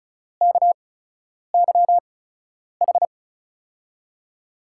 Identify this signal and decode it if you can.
Morse code, heard as a keyed tone, about 34 wpm, spaced as for 10 wpm, 700 Hz. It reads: KYH
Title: USA Call Signs - 8x Character Spacing - 35wpm